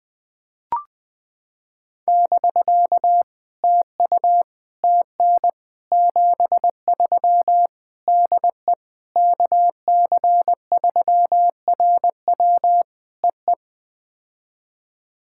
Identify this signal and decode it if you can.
Morse code, heard as a keyed tone, about 20 wpm, 700 Hz. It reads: <BK> TU TN 73 DE KC3RW EE
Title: POTA QSO Practice - 20wpm